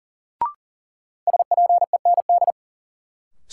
Morse code, heard as a keyed tone, about 40 wpm, 700 Hz. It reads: SPEND
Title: Sentences from Top 800 Words - New Word Drill - 40wpm